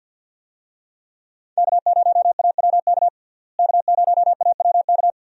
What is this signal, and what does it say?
Morse code, heard as a keyed tone, about 45 wpm, 700 Hz. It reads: K0AWK K0AWK
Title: USA Call Signs - 5 Characters - Repeated 3x - 50wpm